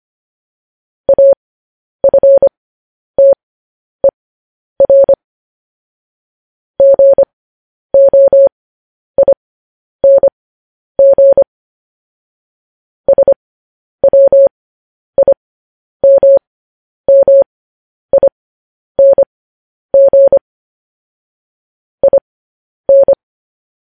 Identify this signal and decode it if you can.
Morse code, heard as a keyed tone, about 25 wpm, spaced as for 10 wpm, 550 Hz. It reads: AFTER GOING SWIMMING IN
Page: news-headlines-in-morse-code-at-10-wpm